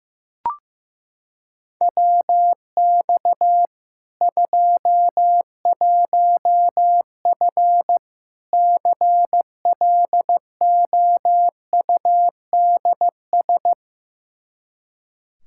Morse code, heard as a keyed tone, about 15 wpm, 700 Hz. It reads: WX 21F CLOUDS
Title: K1USN SST QSO Practice 15wpm